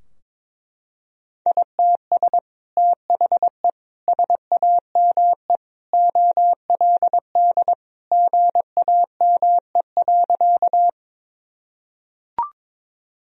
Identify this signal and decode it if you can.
Morse code, heard as a keyed tone, about 22 wpm, 700 Hz. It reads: ITS THE SAME OLD GAME.
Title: Sentences from Top 1000 Words - Review Part 3 - 22wpm